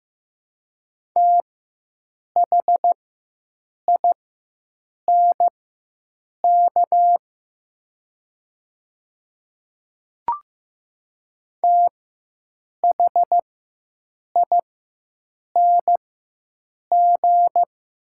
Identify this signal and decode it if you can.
Morse code, heard as a keyed tone, about 15 wpm, spaced as for 7 wpm, 700 Hz. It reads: THINK THING
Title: 4x Character Spacing - Top 200 Words - 15wpm